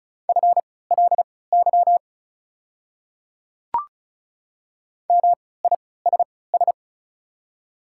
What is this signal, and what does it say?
Morse code, heard as a keyed tone, about 36 wpm, spaced as for 20 wpm, 700 Hz. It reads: FLY MISS
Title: 3x Character Spacing - Top 500 Words - 35wpm